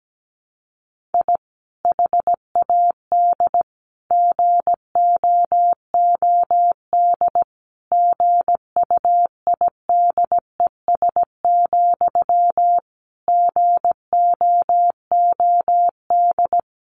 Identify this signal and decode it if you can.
Morse code, heard as a keyed tone, about 17 wpm, 700 Hz. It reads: I HAD GOOD GUIDES, GOOD
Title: A Sound Of Thunder 17wpm